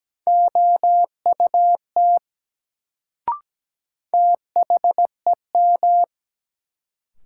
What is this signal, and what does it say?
Morse code, heard as a keyed tone, about 17 wpm, 700 Hz. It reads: OUT THEM
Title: Sentences from Top 100 Words - New Word Drill - 17wpm